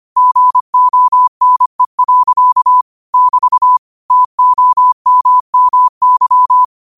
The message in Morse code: GONE. <BT> TOMMY

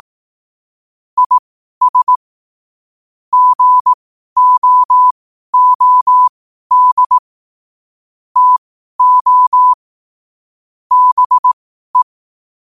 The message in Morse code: IS GOOD TO BE

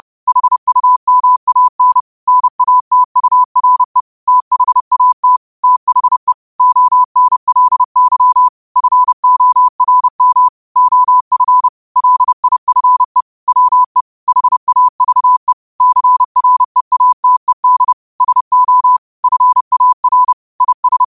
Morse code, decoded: HUMAN NATURE THAT THE ONLY FORM OF LIFE WE HAVE CREATED SO FAR IS

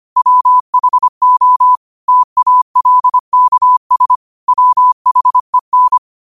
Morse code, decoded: WHO TALKS WHEN

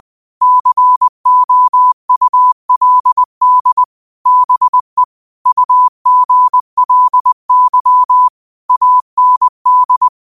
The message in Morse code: COULD BE UGLY AND

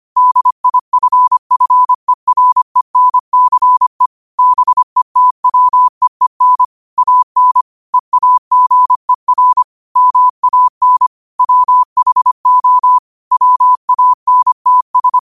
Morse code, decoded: DIFFERENCE BETWEEN AN EAGER MAN WHO WANTS

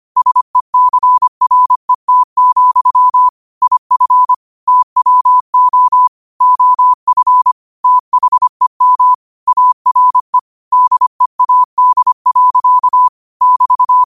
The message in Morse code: SECRET, IF TWO OF THEM ARE DEAD. <BT>

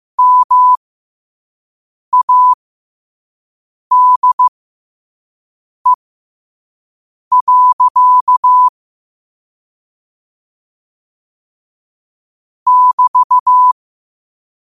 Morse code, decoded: MADE. <BT>

